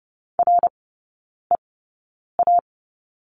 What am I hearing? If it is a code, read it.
Morse code, heard as a keyed tone, about 30 wpm, spaced as for 9 wpm, 690 Hz. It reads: REA
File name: Lesson 17e - Letter u - Three Characters 30wpm